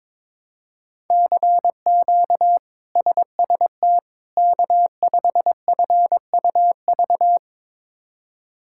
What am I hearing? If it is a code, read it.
Morse code, heard as a keyed tone, about 22 wpm, 700 Hz. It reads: CQ SST K5FUV
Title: K1USN SST QSO Practice - 22wpm